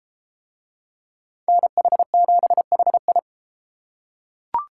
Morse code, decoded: NH7HI

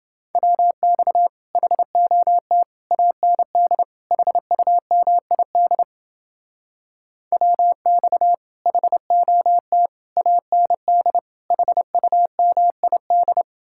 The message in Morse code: WX HOT AND HUMID WX HOT AND HUMID

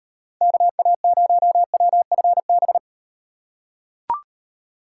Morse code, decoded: KA0WFB

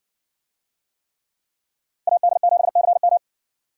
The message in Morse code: AD7CM

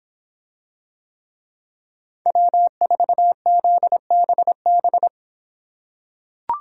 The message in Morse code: W4ZBB

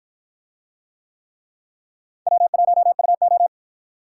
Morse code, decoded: W1UO

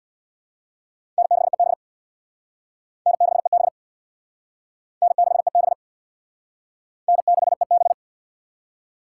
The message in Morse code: N6EB N6EB N6EB N6EB

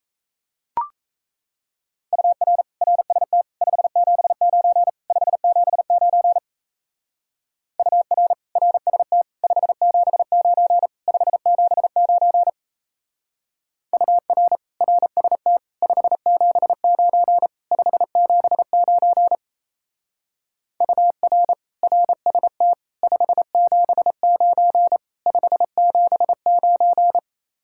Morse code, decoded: UR RST 579 579 UR RST 579 579 UR RST 579 579 UR RST 579 579

